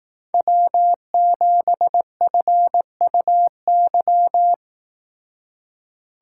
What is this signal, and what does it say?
Morse code, heard as a keyed tone, about 19 wpm, 700 Hz. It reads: W7FUY
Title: Speed Racing - USA 1x3 Call Signs 15wpm